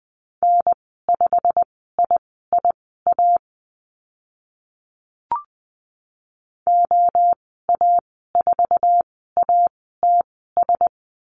Morse code, heard as a keyed tone, about 20 wpm, spaced as for 14 wpm, 700 Hz. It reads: N5IIA OA4ATS
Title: 088 Lesson 15g - Number 5 - Call Signs 20wpm